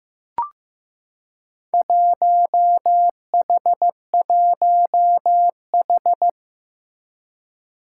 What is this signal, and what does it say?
Morse code, heard as a keyed tone, about 15 wpm, 700 Hz. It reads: 1H1H
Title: Four Challenging Letter-Number Combinations 15wpm